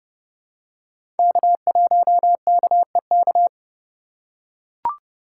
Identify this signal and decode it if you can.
Morse code, heard as a keyed tone, about 30 wpm, 690 Hz. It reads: K1KEK
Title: USA Call Signs - 3x Character Spacing - 30wpm